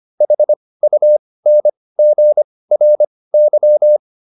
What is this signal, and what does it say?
Morse code, heard as a keyed tone, about 25 wpm, spaced as for 18 wpm, 600 Hz. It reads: HUNGRY